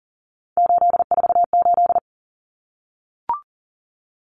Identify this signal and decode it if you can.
Morse code, heard as a keyed tone, about 40 wpm, 700 Hz. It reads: 848
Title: Numbers - 3-Digit 40wpm